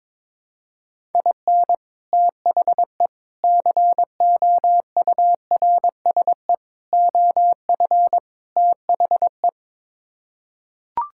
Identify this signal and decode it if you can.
Morse code, heard as a keyed tone, about 22 wpm, 700 Hz. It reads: IN THE COURSE OF THE
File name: practice / 22wpm / Build A Phrase - 5-Word Phrases - 22wpm